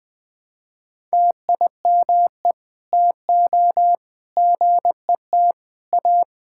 Morse code, heard as a keyed tone, about 20 wpm, 700 Hz. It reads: TIME TO GET A
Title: Sentences from Top 1600 Words 20wpm